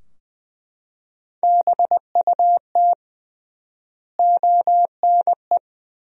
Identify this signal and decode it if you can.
Morse code, heard as a keyed tone, about 20 wpm, 700 Hz. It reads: BUT ONE
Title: 3x Word-Spacing - Sentences from Top 400 Words - 20wpm